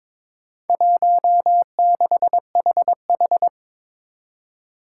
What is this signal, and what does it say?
Morse code, heard as a keyed tone, about 22 wpm, 700 Hz. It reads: 16HH
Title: Four Challenging Letter-Number Combinations 22wpm